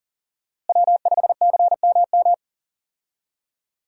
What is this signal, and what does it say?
Morse code, heard as a keyed tone, about 39 wpm, 700 Hz. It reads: W5CMM
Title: USA Call Signs - 5 Characters - Repeated 2x - 40wpm